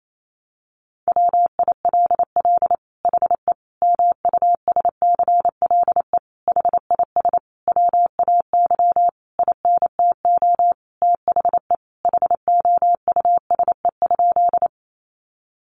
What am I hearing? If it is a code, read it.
Morse code, heard as a keyed tone, about 28 wpm, 700 Hz. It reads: WILL HE MUSCLE HIS WAY INTO THE HOUSE?
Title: Sentences from Top 1300 Words 28wpm